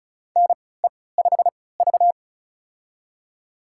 Morse code, heard as a keyed tone, about 36 wpm, spaced as for 20 wpm, 700 Hz. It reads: NE5V
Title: USA Call Signs - 3x Character Spacing - 35wpm